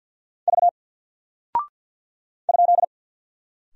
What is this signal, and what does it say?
Morse code, heard as a keyed tone, about 49 wpm, 700 Hz. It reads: V ?
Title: Single Character (Letter:Number:Punctuation:Prosign) Alt 50wpm